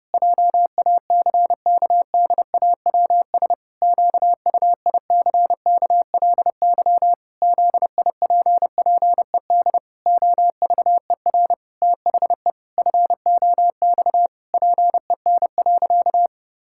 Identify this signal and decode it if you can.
Morse code, heard as a keyed tone, about 30 wpm, 700 Hz. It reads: JACKDAWS QUICKLY ZIPPED OVER THE FOX PEN.
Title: Pangram Challenge v2 - 30wpm